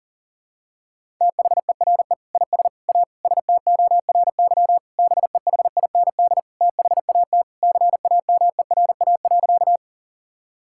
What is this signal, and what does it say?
Morse code, heard as a keyed tone, about 40 wpm, 700 Hz. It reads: THERE IS A STORY BEHIND THAT CAMERA.
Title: Sentences from Top 1000 Words - Review Part 5 - 40wpm